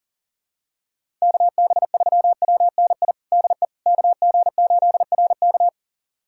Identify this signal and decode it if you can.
Morse code, heard as a keyed tone, about 40 wpm, 700 Hz. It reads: KB3WNI DE KG8RK
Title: USA Callsign DE Callsign 40wpm